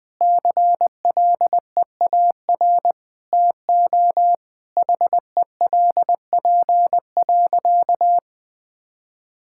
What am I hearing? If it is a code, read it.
Morse code, heard as a keyed tone, about 20 wpm, 700 Hz. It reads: CLEAR TO HELP.
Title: Sentences with 8-Word Phrases - 20wpm